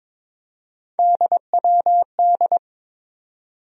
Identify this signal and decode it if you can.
Morse code, heard as a keyed tone, about 22 wpm, 700 Hz. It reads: DWD
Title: USA Call Sign Postfixes - 3 Letters 22wpm